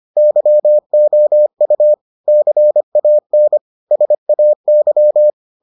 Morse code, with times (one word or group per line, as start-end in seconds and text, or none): 0.2-1.9 YOU
2.3-3.6 CAN
3.9-5.3 SAY